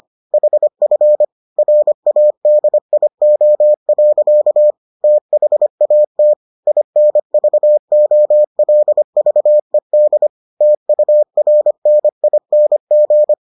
Morse code, heard as a keyed tone, about 25 wpm, 600 Hz. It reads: HF RADIO. THAT INVOLVED TURNING